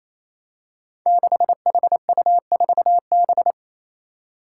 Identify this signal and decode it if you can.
Morse code, heard as a keyed tone, about 28 wpm, 700 Hz. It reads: 6HU4B
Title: Five Challenging Letter-Number Combinations 28wpm